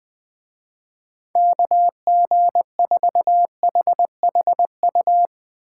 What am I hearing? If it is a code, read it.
Morse code, heard as a keyed tone, about 20 wpm, 700 Hz. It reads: KG4HHU